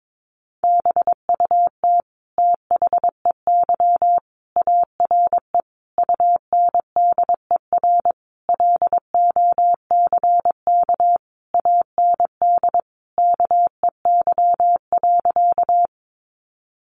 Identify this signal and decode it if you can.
Morse code, heard as a keyed tone, about 22 wpm, 700 Hz. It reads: BUT THEY ARE UNDER LOCK AND KEY.
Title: Sentences from Top 1300 Words 22wpm